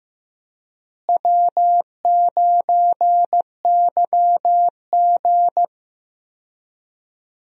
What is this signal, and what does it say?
Morse code, heard as a keyed tone, about 15 wpm, 700 Hz. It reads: W9YG
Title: USA Call Signs - 4 Characters - Repeated 2x - 15wpm